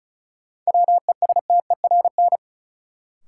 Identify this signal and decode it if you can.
Morse code, heard as a keyed tone, about 34 wpm, 700 Hz. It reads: WESTERN
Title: Sentences from Top 1000 Words - Review Part 7 - New Word Drill - 35wpm